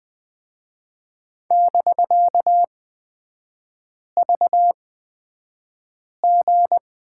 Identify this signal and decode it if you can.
Morse code, heard as a keyed tone, about 20 wpm, spaced as for 5 wpm, 690 Hz. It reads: <BK>VG
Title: 255 Lesson 41e - Period - Three Characters 20wpm